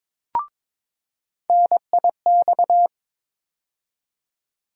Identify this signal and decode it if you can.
Morse code, heard as a keyed tone, about 22 wpm, 700 Hz. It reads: NIX
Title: USA Call Sign Postfixes - 3 Letters 22wpm